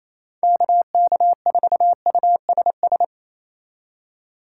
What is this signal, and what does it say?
Morse code, heard as a keyed tone, about 36 wpm, 700 Hz. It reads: KK4USS